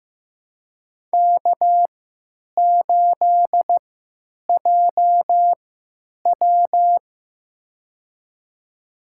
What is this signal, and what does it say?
Morse code, heard as a keyed tone, about 15 wpm, spaced as for 9 wpm, 700 Hz. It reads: K8JW
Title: USA Call Signs - 4 Characters - 3x Character Spacing - 15wpm